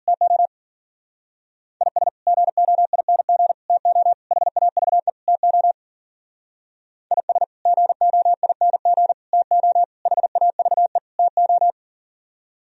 Morse code, heard as a keyed tone, about 56 wpm, 700 Hz. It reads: TO IS GOING TO HAVE TO IS GOING TO HAVE TO